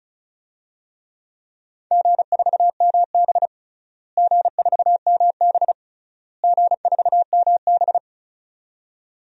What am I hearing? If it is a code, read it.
Morse code, heard as a keyed tone, about 36 wpm, 690 Hz. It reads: G4MB G4MB G4MB
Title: DXCC-100 Call Signs - 4 Characters - Repeated 3x - 35wpm